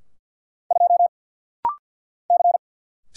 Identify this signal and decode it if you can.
Morse code, heard as a keyed tone, about 50 wpm, 700 Hz. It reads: 2 C
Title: Single Letter-Number Alt - Rapid-Fire 50wpm